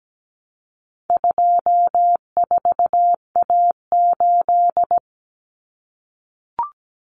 Morse code, heard as a keyed tone, about 17 wpm, 700 Hz. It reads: 24A8
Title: Common Cut Numbers - 4-Digit 17wpm